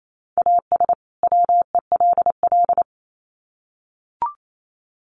Morse code, heard as a keyed tone, about 28 wpm, 700 Hz. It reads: AS WELL
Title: Sets of 2 Words 28wpm